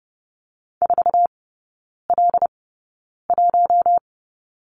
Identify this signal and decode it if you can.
Morse code, heard as a keyed tone, about 30 wpm, spaced as for 9 wpm, 700 Hz. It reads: 4L1
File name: Lesson 19e - Letter w - Three Characters 30wpm